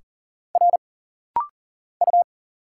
Morse code, heard as a keyed tone, about 40 wpm, 700 Hz. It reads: R U
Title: Single Letters - Rapid-Fire 40wpm